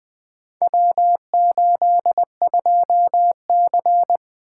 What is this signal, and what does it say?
Morse code, heard as a keyed tone, about 20 wpm, 700 Hz. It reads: W82C
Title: Four Letter-Number Combinations - 20wpm